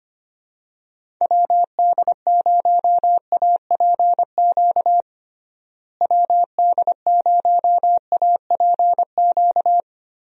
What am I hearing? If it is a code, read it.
Morse code, heard as a keyed tone, about 25 wpm, 700 Hz. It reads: WD0APQ WD0APQ
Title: USA Call Signs - 6 Characters - Repeated 3x - 25wpm